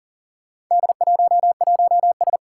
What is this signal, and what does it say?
Morse code, heard as a keyed tone, about 39 wpm, 700 Hz. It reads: D11S
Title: Four Challenging Letter-Number Combinations 40wpm